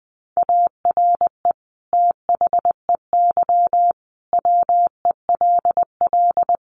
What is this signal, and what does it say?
Morse code, heard as a keyed tone, about 20 wpm, 700 Hz. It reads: ARE THEY WELL
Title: Sentences from Top 1300 Words 20wpm